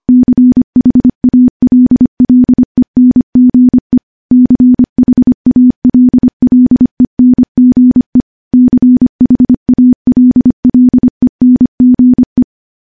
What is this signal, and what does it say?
Morse code, heard as a keyed tone, about 25 wpm, 250 Hz. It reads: CHALLENGE CHALLENGE CHALLENGE
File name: morse-training-25WPM